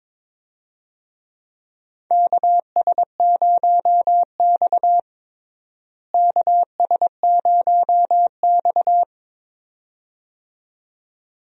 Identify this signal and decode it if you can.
Morse code, heard as a keyed tone, about 22 wpm, 700 Hz. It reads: KS0X KS0X
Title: USA Call Signs - 4 Characters - Repeated 2x - 22wpm